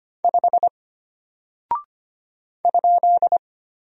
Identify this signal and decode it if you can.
Morse code, heard as a keyed tone, about 25 wpm, 700 Hz. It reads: 5 ?
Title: Single Character (Letter:Number:Punctuation:Prosign) Alt 25wpm